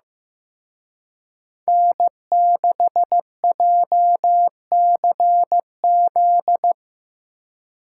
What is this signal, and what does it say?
Morse code, heard as a keyed tone, about 18 wpm, 700 Hz. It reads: N6JCZ
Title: Speed Racing - USA 1x3 Call Signs 15wpm